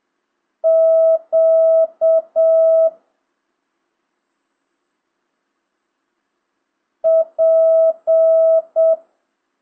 Morse code, heard as a keyed tone, about 7 wpm, 650 Hz. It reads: Q P
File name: jh8cbh / 2025cw